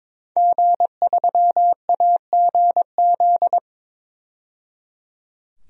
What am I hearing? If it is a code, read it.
Morse code, heard as a keyed tone, about 22 wpm, 700 Hz. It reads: G3AGZ
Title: DXCC-100 Call Signs - 5 Characters - 22wpm